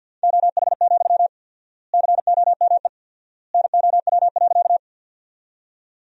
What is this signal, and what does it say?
Morse code, heard as a keyed tone, about 50 wpm, 700 Hz. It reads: OH, COME NOW.
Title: 4x Word-Spacing - Sentences from Top 500 Words - 50wpm